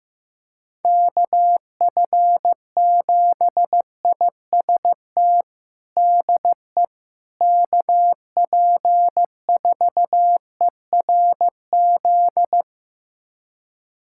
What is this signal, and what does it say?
Morse code, heard as a keyed tone, about 16 wpm, 700 Hz. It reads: KF7IST DE KP4ERZ
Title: USA Callsign DE Callsign - 6 Characters - 15wpm